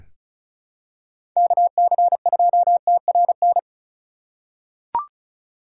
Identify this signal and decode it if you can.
Morse code, heard as a keyed tone, about 36 wpm, 700 Hz. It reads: KC2TRN
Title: USA Call Signs - 6 Characters - 3x Character Spacing - 35wpm